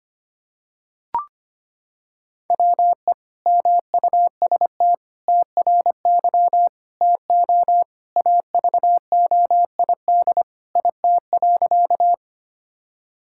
Morse code, heard as a keyed tone, about 25 wpm, 700 Hz. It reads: WE MUST TRY TO AVOID IT.
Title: Sentences from Top 1000 Words - Review Part 6 - 25wpm